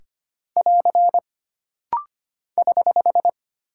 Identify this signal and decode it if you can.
Morse code, heard as a keyed tone, about 25 wpm, 700 Hz. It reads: <AR> <HH>
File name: Single Punctuation-Prosign - Rapid-Fire - 25wpm